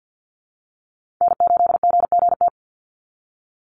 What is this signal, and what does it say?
Morse code, heard as a keyed tone, about 54 wpm, 700 Hz. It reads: N8GGT
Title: USA Call Signs 50wpm